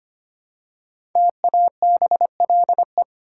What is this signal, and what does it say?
Morse code, heard as a keyed tone, about 25 wpm, 700 Hz. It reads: TABLE
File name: 5 Letter Words - First 3 Letters Given - 25wpm